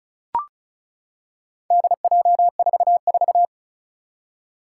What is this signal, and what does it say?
Morse code, heard as a keyed tone, about 36 wpm, 700 Hz. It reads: DJ44
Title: Four Challenging Letter-Number Combinations 35wpm